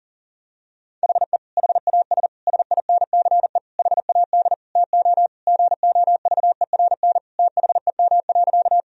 Morse code, heard as a keyed tone, about 40 wpm, 700 Hz. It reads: HE HAS SINCE HAD TO GOVERN THEM.
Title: Sentences from Top 1000 Words - Review Part 2 - 40wpm